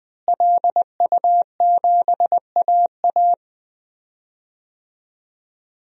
Morse code, heard as a keyed tone, about 20 wpm, 700 Hz. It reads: LU7AA
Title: DXCC-100 Call Signs - 5 Characters - Repeated 3x - 20wpm